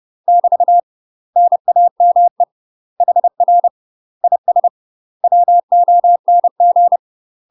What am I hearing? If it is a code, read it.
Morse code, heard as a keyed tone, about 30 wpm, 700 Hz. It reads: <BT> NAME HR IS WONG
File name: UK MASTER QSOs TRAINING / 30WPM / UK MASTER QSOs 0008